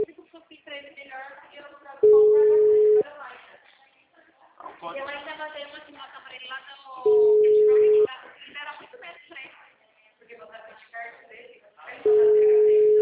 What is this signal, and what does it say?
Ringback tone (Europe, 425 Hz), with speech.